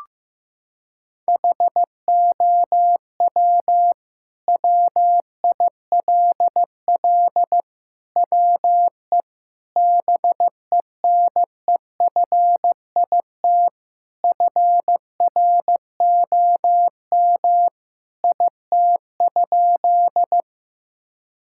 Morse code, heard as a keyed tone, about 15 wpm, 700 Hz. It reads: HOW WILL WE BENEFIT FROM IT?
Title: Sentences from Top 1000 Words - Review Part 6 - 15wpm